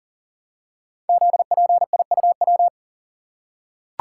Morse code, heard as a keyed tone, about 43 wpm, 700 Hz. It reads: ZPIUW
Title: Code Groups - 5 Letters - 40wpm